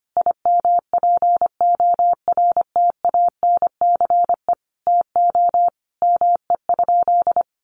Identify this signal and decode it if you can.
Morse code, heard as a keyed tone, about 25 wpm, 700 Hz. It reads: IMPORTANCE TO ME?
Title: Sentences from Top 1200 Words 25wpm